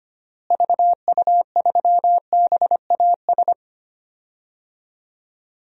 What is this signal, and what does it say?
Morse code, heard as a keyed tone, about 25 wpm, 700 Hz. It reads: VU3BAS
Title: DXCC-100 Call Signs - 6 Characters - Repeated 2x - 25wpm